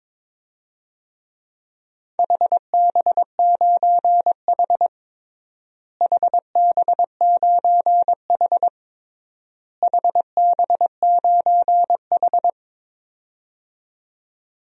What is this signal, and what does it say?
Morse code, heard as a keyed tone, about 22 wpm, 700 Hz. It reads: HB9H HB9H HB9H
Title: DXCC-100 Call Signs - 4 Characters - Repeated 3x - 22wpm